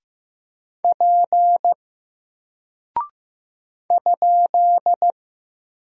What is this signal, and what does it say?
Morse code, heard as a keyed tone, about 15 wpm, 700 Hz. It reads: P ?
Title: Single Character (Letter:Number:Punctuation:Prosign) Alt 15wpm